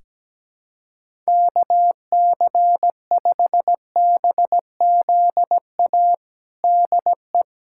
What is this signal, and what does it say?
Morse code, heard as a keyed tone, about 17 wpm, 700 Hz. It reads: KC5BZA DE
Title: USA Callsign DE Callsign 17wpm